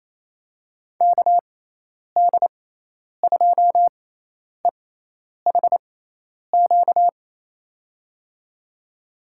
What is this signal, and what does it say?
Morse code, heard as a keyed tone, about 28 wpm, spaced as for 10 wpm, 700 Hz. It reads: KD2EHQ
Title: USA Call Signs - 6 Characters - 6x Character Spacing - 28wpm